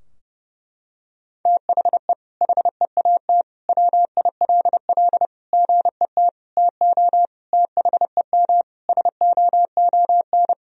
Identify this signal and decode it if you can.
Morse code, heard as a keyed tone, about 30 wpm, 700 Hz. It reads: THE HEAT WILL GET TO THEM SOON
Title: Sentences from Top 1000 Words - Review Part 3 - 30wpm